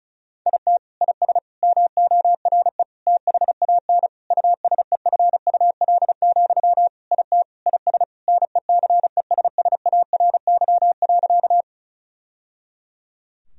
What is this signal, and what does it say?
Morse code, heard as a keyed tone, about 34 wpm, 700 Hz. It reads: IT IS MORE THAN USEFUL, IT IS NECESSARY.
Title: Sentences from Top 1000 Words - Review Part 4 - 35wpm